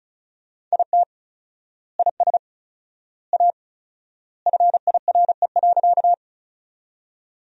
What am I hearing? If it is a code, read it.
Morse code, heard as a keyed tone, about 36 wpm, 700 Hz. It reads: IT IS A FIRE.
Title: 4x Word-Spacing - Sentences from Top 400 Words - 35wpm